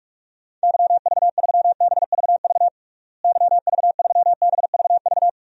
Morse code, heard as a keyed tone, about 46 wpm, 700 Hz. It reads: YV3BVV YV3BVV
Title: DXCC-100 Call Signs - 6 Characters - Repeated 3x - 45wpm